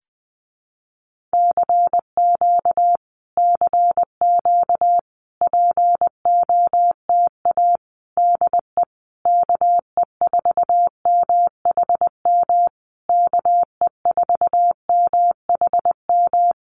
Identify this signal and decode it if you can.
Morse code, heard as a keyed tone, about 20 wpm, 700 Hz. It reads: CQ CQ POTA DE KE4MHM KE4MHM
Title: Common QSO Phrases Encore 20wpm